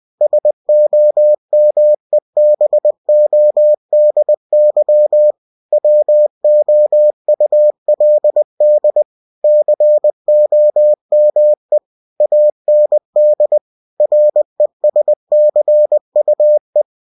Morse code, heard as a keyed tone, about 20 wpm, 600 Hz. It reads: SOMEBODY WOULD COME AND RESCUE
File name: cw-books / winnie-the-pooh / winnie-20wpm-page-0092